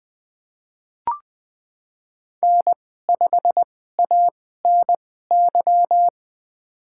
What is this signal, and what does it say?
Morse code, heard as a keyed tone, about 20 wpm, spaced as for 15 wpm, 700 Hz. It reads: N5ANY
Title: USA Call Signs - 5 Characters - 2x Character Spacing - 20wpm